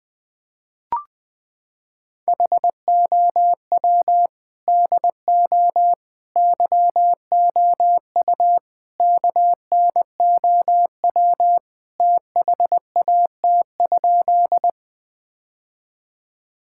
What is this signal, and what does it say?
Morse code, heard as a keyed tone, about 21 wpm, 700 Hz. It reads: HOW DO YOU KNOW THAT?
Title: Sentences with 4-Word Phrases - 20wpm